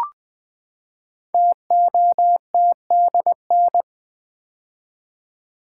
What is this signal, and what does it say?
Morse code, heard as a keyed tone, about 20 wpm, 700 Hz. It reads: TOTDN
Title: Code Groups - 5 Letters - 20wpm